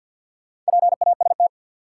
Morse code, heard as a keyed tone, about 53 wpm, 700 Hz. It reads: PAST
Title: Sentences from Top 600 Words - New Word Drill - 50wpm